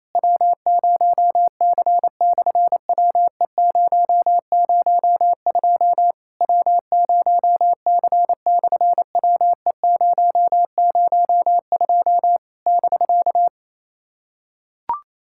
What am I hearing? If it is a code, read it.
Morse code, heard as a keyed tone, about 28 wpm, 700 Hz. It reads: W0C/WE002 W0C/WE002 <BK>
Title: SOTA QSO Practice 28wpm